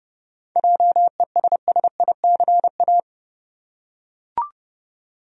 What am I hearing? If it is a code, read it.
Morse code, heard as a keyed tone, about 30 wpm, 700 Hz. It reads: JESSICA
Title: USA Common Names - 30wpm